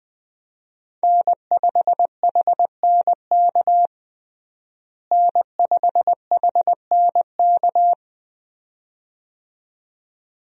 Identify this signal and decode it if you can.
Morse code, heard as a keyed tone, about 20 wpm, 700 Hz. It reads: N5HNK N5HNK
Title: USA Call Signs - 5 Characters - Repeated 2x - 20wpm